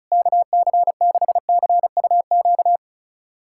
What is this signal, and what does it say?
Morse code, heard as a keyed tone, about 34 wpm, 700 Hz. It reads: KC6CUQ
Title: USA Callsign DE Callsign 35wpm